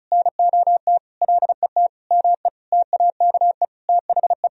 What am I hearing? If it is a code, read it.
Morse code, heard as a keyed tone, about 34 wpm, 700 Hz. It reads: NOT LET ME TAKE THE
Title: Sentences from Top 1000 Words - Review Part 2 - 35wpm